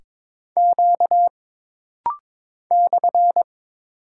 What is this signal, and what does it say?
Morse code, heard as a keyed tone, about 22 wpm, 700 Hz. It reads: Q /
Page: This rapid-fire timing is geared to push your ICR (Instant Character Recognition) skills.